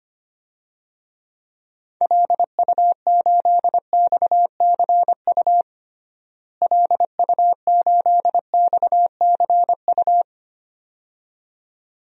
Morse code, heard as a keyed tone, about 25 wpm, 700 Hz. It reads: LU8XCU LU8XCU
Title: DXCC-100 Call Signs - 6 Characters - Repeated 2x - 25wpm